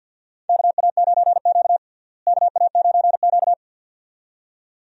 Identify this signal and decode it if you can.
Morse code, heard as a keyed tone, about 47 wpm, 700 Hz. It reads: KA9Q KA9Q
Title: USA Call Signs - 4 Characters - Repeated 2x - 50wpm